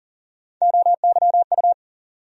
Morse code, heard as a keyed tone, about 40 wpm, 700 Hz. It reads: OYU
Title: USA Call Sign Postfixes - 3 Letters 40wpm